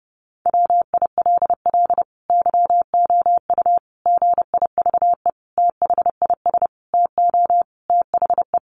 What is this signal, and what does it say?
Morse code, heard as a keyed tone, about 30 wpm, 700 Hz. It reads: WILL YOU GIVE THIS TO THE
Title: Sentences from Top 200 Words 30wpm